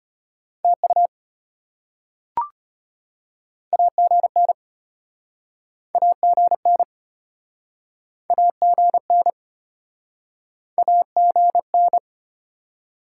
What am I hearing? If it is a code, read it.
Morse code, heard as a keyed tone, about 33 wpm, 700 Hz. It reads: TU AGN AGN AGN AGN